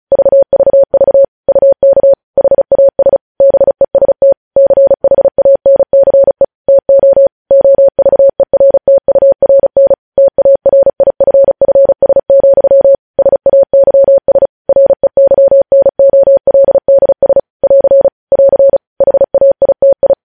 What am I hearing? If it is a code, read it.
Morse code, heard as a keyed tone, about 36 wpm, 550 Hz. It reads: VVV UK HAS BEST CHANCE TO OVERTURN TARIFFS, SAYS REYNOLDS <AR> <AR> HAITI
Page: news headlines in morse code at 35 wpm